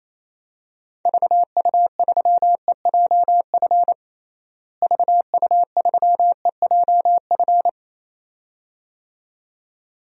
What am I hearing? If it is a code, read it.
Morse code, heard as a keyed tone, about 28 wpm, 700 Hz. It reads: VU3EJF VU3EJF
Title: DXCC-100 Call Signs - 6 Characters - Repeated 2x - 28wpm